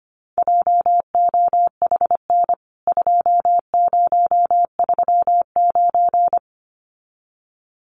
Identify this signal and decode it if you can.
Morse code, heard as a keyed tone, about 25 wpm, 700 Hz. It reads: JOHN 2039
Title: CWT Exchange 25wpm